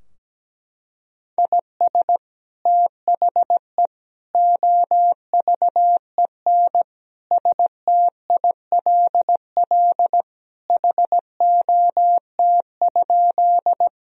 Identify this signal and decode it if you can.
Morse code, heard as a keyed tone, about 17 wpm, 700 Hz. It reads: IS THE OVEN STILL HOT?
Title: Sentences from Top 1000 Words - Review Part 4 - 17wpm